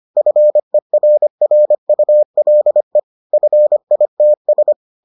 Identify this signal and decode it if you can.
Morse code, heard as a keyed tone, about 25 wpm, 600 Hz. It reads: FERRULE FITS